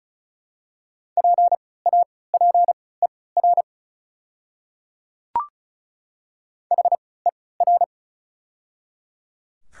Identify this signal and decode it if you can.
Morse code, heard as a keyed tone, about 35 wpm, spaced as for 20 wpm, 700 Hz. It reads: PAPER HER
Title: 3x Character Spacing - Top 500 Words - 35wpm